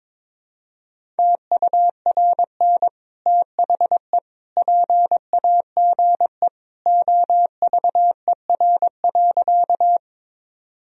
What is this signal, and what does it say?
Morse code, heard as a keyed tone, about 22 wpm, 700 Hz. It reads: TURN THE PAGE OVER.
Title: Sentences from Top 1000 Words - Review Part 1 - 22wpm